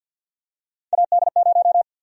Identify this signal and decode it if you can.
Morse code, heard as a keyed tone, about 53 wpm, 700 Hz. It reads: AD0
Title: USA Call Sign Prefixes - 50wpm